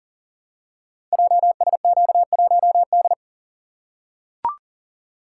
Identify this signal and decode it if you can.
Morse code, heard as a keyed tone, about 39 wpm, 700 Hz. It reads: JSQ1D
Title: Code Groups - 5 Letter:Numbers - 40wpm